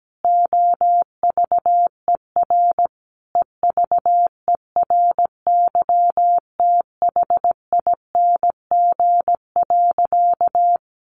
Morse code, heard as a keyed tone, about 17 wpm, 700 Hz. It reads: OVER EVERYTHING.
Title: Sentences from Top 900 Words 17wpm